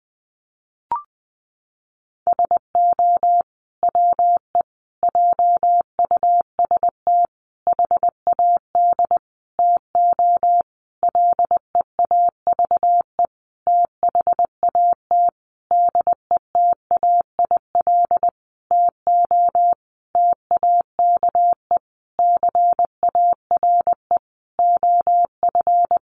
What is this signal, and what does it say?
Morse code, heard as a keyed tone, about 20 wpm, 700 Hz. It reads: SO WE JUST HAD TO LEAVE THAT DETAIL TO TAKE CARE OF
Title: Sentences from Top 1500 Words 20wpm